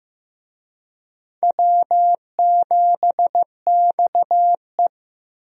Morse code, heard as a keyed tone, about 15 wpm, 700 Hz. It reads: W7XE
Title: USA Call Signs - 4 Characters - Repeated 3x - 15wpm